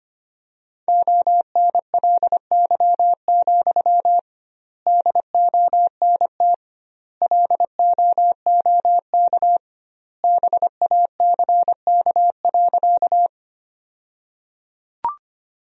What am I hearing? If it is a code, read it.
Morse code, heard as a keyed tone, about 25 wpm, 700 Hz. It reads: ONLY, DONT LOOK BACK.
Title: 2x Word-Spacing - Sentences from Top 300 Words - 25wpm